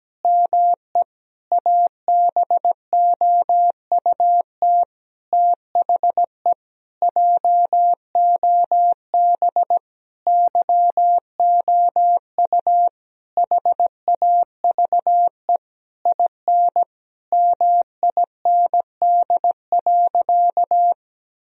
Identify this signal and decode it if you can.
Morse code, heard as a keyed tone, about 17 wpm, 700 Hz. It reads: ME ABOUT THE JOB YOU HAVE IN MIND.
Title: Sentences from Top 1000 Words - Review Part 3 - 17wpm